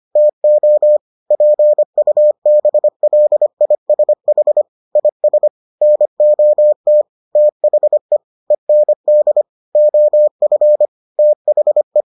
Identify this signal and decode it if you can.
Morse code, heard as a keyed tone, about 25 wpm, 600 Hz. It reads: TO PUBLISH IS NOT THE END OF THE